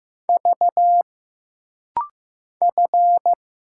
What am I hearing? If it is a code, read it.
Morse code, heard as a keyed tone, about 15 wpm, 700 Hz. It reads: V F
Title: Single Letters - Rapid-Fire 15wpm